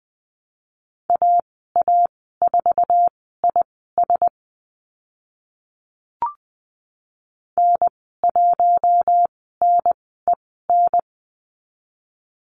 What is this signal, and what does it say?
Morse code, heard as a keyed tone, about 20 wpm, spaced as for 15 wpm, 700 Hz. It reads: AA4IS N1NEN
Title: 048 Lesson 9g - Number 4 - Call Signs 20wpm